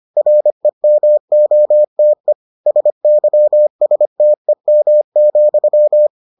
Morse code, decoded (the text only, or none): REMOTE SYSTEM,